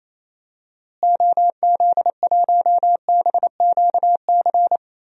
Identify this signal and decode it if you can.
Morse code, heard as a keyed tone, about 28 wpm, 700 Hz. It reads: OZ1BQC
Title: DXCC-100 Call Signs - 6 Characters - Repeated 2x - 28wpm